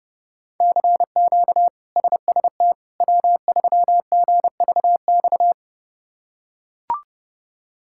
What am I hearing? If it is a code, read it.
Morse code, heard as a keyed tone, about 29 wpm, 700 Hz. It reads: CQ SST W3GVX
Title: K1USN SST QSO Practice - 30wpm